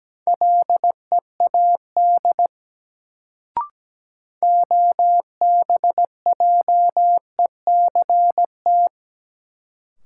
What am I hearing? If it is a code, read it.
Morse code, heard as a keyed tone, about 17 wpm, 700 Hz. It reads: LEAD OBJECT
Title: Sentences from Top 500 Words - New Word Drill - 17wpm